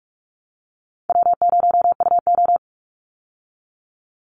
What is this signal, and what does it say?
Morse code, heard as a keyed tone, about 44 wpm, 700 Hz. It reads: W0UO
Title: Easy Call Signs Repeated 2x - 45wpm